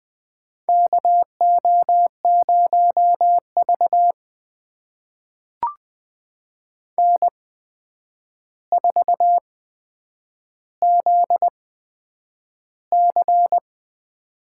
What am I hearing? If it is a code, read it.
Morse code, heard as a keyed tone, about 20 wpm, spaced as for 9 wpm, 700 Hz. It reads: KO0V N4ZC
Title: USA Call Signs - 4 Characters - 8x Character Spacing - 20wpm